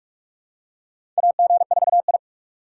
Morse code, heard as a keyed tone, about 47 wpm, 700 Hz. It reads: AG4I
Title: USA Call Signs - 4 Characters - 3x Character Spacing - 45wpm